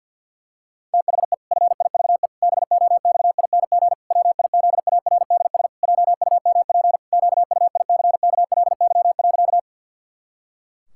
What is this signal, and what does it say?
Morse code, heard as a keyed tone, about 53 wpm, 700 Hz. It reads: THE FIVE BOXING WIZARDS JUMP QUICKLY.
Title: Pangram Challenge v2 - 50wpm